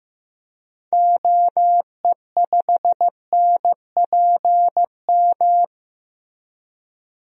Morse code, heard as a keyed tone, about 15 wpm, 700 Hz. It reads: OE5NPM
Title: DXCC-100 Call Signs - 6 Characters - 15wpm